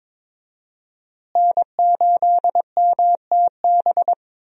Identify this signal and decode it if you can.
Morse code, heard as a keyed tone, about 22 wpm, 700 Hz. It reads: N8MTB
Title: USA Call Signs - 5 Characters - Repeated 3x - 22wpm